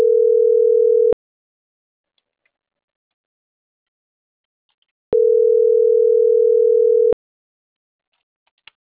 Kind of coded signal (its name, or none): telephone call progress tone